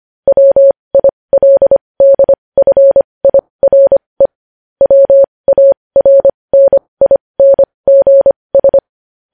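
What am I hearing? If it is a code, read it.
Morse code, heard as a keyed tone, about 25 wpm, spaced as for 20 wpm, 550 Hz. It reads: WILDFIRE WARNINGS
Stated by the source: Morse code news headlines practice at 20 wpm.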